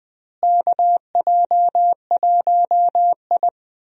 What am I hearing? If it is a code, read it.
Morse code, heard as a keyed tone, about 21 wpm, 700 Hz. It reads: KJ1I
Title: USA Call Signs - 4 Characters - Repeated 3x - 20wpm